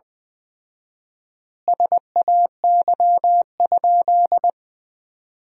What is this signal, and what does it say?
Morse code, heard as a keyed tone, about 19 wpm, 700 Hz. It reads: SAY?
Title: 4x Word-Spacing - Sentences from Top 200 Words - 20wpm